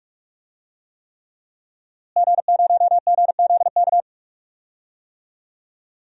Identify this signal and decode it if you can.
Morse code, heard as a keyed tone, about 44 wpm, 700 Hz. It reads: G0GZK
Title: DXCC-100 Call Signs - 5 Characters - Repeated 3x - 45wpm